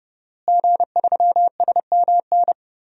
Morse code, heard as a keyed tone, about 31 wpm, 700 Hz. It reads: G3SMN
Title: DXCC-100 Call Signs - 5 Characters - Repeated 3x - 30wpm